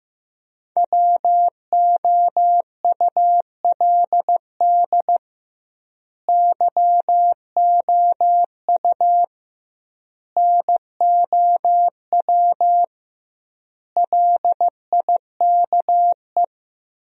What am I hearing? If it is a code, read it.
Morse code, heard as a keyed tone, about 15 wpm, 700 Hz. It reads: WOULD YOU NOW LIKE
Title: 2x Word-Spacing - Sentences from Top 200 Words - 15wpm